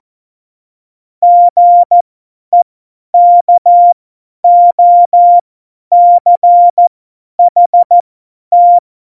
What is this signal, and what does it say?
Morse code, heard as a keyed tone, about 14 wpm, spaced as for 10 wpm, 700 Hz. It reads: GEKOCHT
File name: Morsekurs / Witze / Witz 07. November 2017, 50 F70